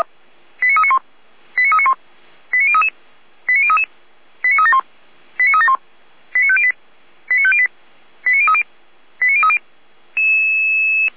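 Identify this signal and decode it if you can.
Five-tone selective call (ZVEI), heard as quick runs of five tones, tones of about 70 ms. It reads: 89391 89391 89033 89033 89371 89371 89508 89508 89030 89030
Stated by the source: ZVEI five-tone call received via SDR.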